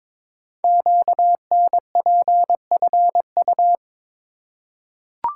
Morse code, heard as a keyed tone, about 22 wpm, 700 Hz. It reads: QNPFU
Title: Code Groups - 5 Letters - 22wpm